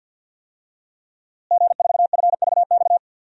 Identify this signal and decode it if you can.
Morse code, heard as a keyed tone, about 52 wpm, 700 Hz. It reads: G4FVX
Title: DXCC-100 Call Signs - 5 Characters - Repeated 2x - 50wpm